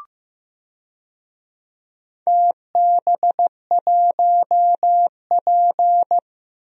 Morse code, heard as a keyed tone, about 15 wpm, 700 Hz. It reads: TB1P
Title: DXCC-100 Call Signs - 4 Characters - Repeated 3x - 15wpm